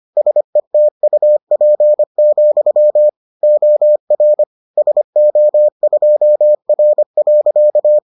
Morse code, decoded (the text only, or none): SETUP, OR SO2R.